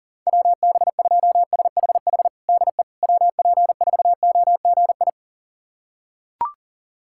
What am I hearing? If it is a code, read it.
Morse code, heard as a keyed tone, about 40 wpm, 700 Hz. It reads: WB2SHH DE WP4OGI
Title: USA Callsign DE Callsign - 6 Characters - 40wpm